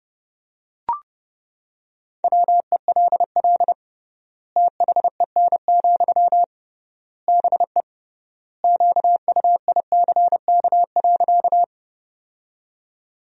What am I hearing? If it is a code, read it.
Morse code, heard as a keyed tone, about 30 wpm, 700 Hz. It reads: WELL THEN, BE QUICK.
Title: 3x Word-Spacing - Sentences from Top 500 Words - 30wpm